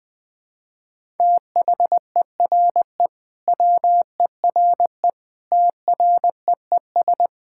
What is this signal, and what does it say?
Morse code, heard as a keyed tone, about 20 wpm, 700 Hz. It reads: THERE WERE TREES
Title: Sentences with 5-Word Phrases - 20wpm